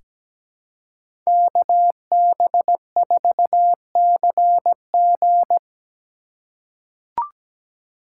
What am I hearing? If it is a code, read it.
Morse code, heard as a keyed tone, about 17 wpm, 700 Hz. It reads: KB4CG
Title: K1USN SST QSO Practice - 17wpm